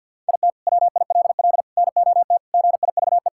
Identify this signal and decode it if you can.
Morse code, heard as a keyed tone, about 50 wpm, 700 Hz. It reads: IT WILL NOT GIVE
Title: Sentences from Top 1600 Words 50wpm